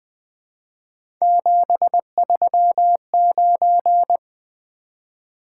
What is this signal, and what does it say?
Morse code, heard as a keyed tone, about 20 wpm, 700 Hz. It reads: 739
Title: Three Letter-Number Combinations - 20wpm